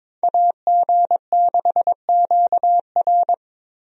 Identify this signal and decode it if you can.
Morse code, heard as a keyed tone, about 22 wpm, 700 Hz. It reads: AG6QR
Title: USA Call Signs - 5 Characters - Repeated 3x - 22wpm